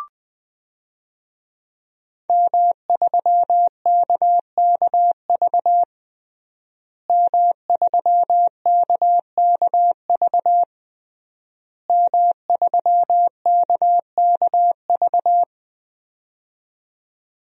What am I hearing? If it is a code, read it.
Morse code, heard as a keyed tone, about 21 wpm, 700 Hz. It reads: M3KKV M3KKV M3KKV
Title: DXCC-100 Call Signs - 5 Characters - Repeated 3x - 20wpm